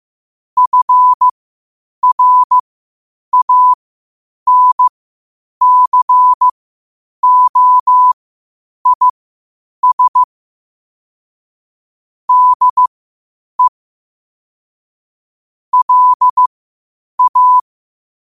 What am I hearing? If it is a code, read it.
Morse code, heard as a keyed tone, about 15 wpm, spaced as for 8 wpm, 1000 Hz. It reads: FRANCOIS DE LA
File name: qotdfiles / 8wpm